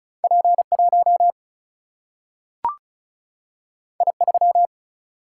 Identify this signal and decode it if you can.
Morse code, heard as a keyed tone, about 36 wpm, 700 Hz. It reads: P 1 I 3
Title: Single-Letter Single-Number 35wpm